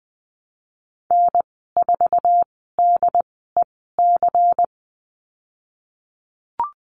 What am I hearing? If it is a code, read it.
Morse code, heard as a keyed tone, about 20 wpm, spaced as for 15 wpm, 700 Hz. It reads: N4DEC
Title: 231 Lesson 37g - Letter x - Call Signs 20wpm